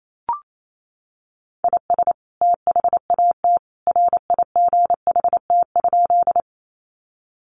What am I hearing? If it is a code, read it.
Morse code, heard as a keyed tone, about 28 wpm, 700 Hz. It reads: IS THAT RIGHT?
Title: Sentences from Top 200 Words 28wpm